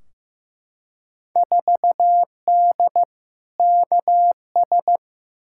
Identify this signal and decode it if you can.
Morse code, heard as a keyed tone, about 15 wpm, 700 Hz. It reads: 4D KS
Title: Field Day Exchanges Encore - 15wpm